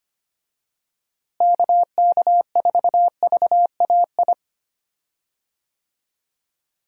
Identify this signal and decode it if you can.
Morse code, heard as a keyed tone, about 25 wpm, 700 Hz. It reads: KK4VAI
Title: USA Call Signs - 6 Characters - Repeated 2x - 25wpm